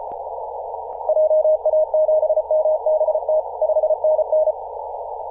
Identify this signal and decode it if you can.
Morse code, heard as a keyed tone, about 34 wpm, 620 Hz, with noise. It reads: JA7MBT 5NN
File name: posts / 2017-aug-07-0652z-ri0li-14-036mhz-cw